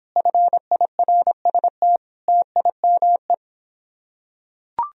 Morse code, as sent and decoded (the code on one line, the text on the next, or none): ..-. .. .-. ... - / - .. -- .
FIRST TIME